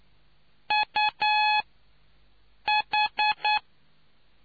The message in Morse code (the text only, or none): UH